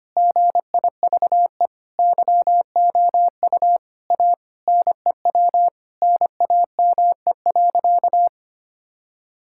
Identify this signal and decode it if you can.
Morse code, heard as a keyed tone, about 25 wpm, 700 Hz. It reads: GIVE YOU A NEW NAME.
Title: Sentences from Top 1000 Words - Review Part 5 - 25wpm